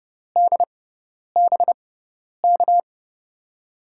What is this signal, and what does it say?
Morse code, heard as a keyed tone, about 30 wpm, spaced as for 10 wpm, 700 Hz. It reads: DBK
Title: USA Call Signs - 6x Character Spacing - 30wpm